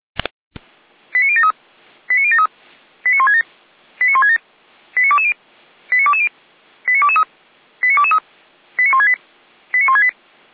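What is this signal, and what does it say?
Five-tone selective call (ZVEI), heard as quick runs of five tones, tones of about 70 ms. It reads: 89073 89073 89167 89167 89220 89220 89223 89223 89168 89168
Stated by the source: ZVEI five-tone selective call received by SDR.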